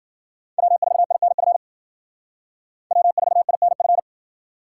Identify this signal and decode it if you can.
Morse code, heard as a keyed tone, about 48 wpm, 700 Hz. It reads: W4INF W4INF
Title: Speed Racing - USA 1x3 Call Signs 40wpm